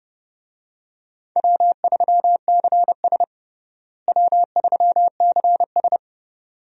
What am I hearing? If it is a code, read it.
Morse code, heard as a keyed tone, about 31 wpm, 700 Hz. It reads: W3CS W3CS
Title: USA Call Signs - 4 Characters - Repeated 2x - 30wpm